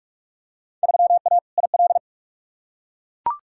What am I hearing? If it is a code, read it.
Morse code, heard as a keyed tone, about 43 wpm, 700 Hz. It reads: 3A IL
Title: Field Day Exchanges - 45wpm